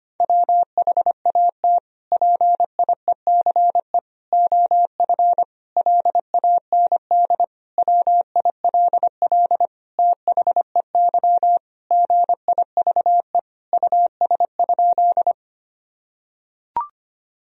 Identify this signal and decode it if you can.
Morse code, heard as a keyed tone, about 25 wpm, 700 Hz. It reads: WHAT PIECE OF LAND WILL THEY GIVE US?
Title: Sentences from Top 1000 Words - Review Part 2 - 25wpm